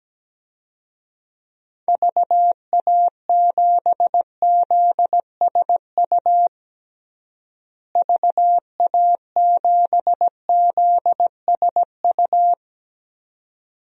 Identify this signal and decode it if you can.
Morse code, heard as a keyed tone, about 17 wpm, 700 Hz. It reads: VA7ZSU VA7ZSU
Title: DXCC-100 Call Signs - 6 Characters - Repeated 2x - 17wpm